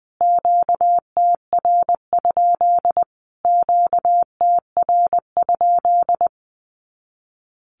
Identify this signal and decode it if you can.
Morse code, heard as a keyed tone, about 20 wpm, 700 Hz. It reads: QTR? QTR?
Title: Q-Codes Repeated 3x 20wpm